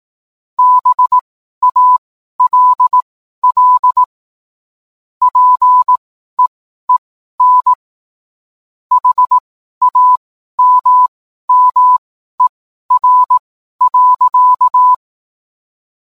Morse code, heard as a keyed tone, about 18 wpm, spaced as for 12 wpm, 1000 Hz. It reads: BALL PEEN HAMMER.